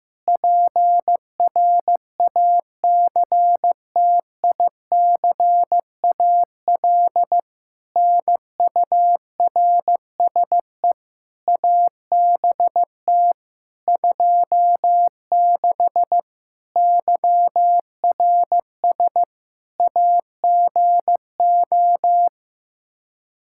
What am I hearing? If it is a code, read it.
Morse code, heard as a keyed tone, about 15 wpm, 700 Hz. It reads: PRACTICAL NURSE ABT 26 YRS AGO
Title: Practice QSO - Exchange 3 - More Info - 15wpm